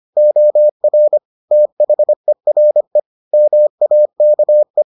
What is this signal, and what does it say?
Morse code, heard as a keyed tone, about 25 wpm, 600 Hz. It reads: OR THERE MAKE